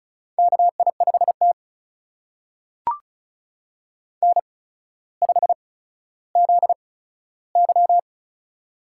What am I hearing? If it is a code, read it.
Morse code, heard as a keyed tone, about 36 wpm, spaced as for 15 wpm, 700 Hz. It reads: KI5T N5ZY
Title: USA Call Signs - 4 Characters - 8x Character Spacing - 35wpm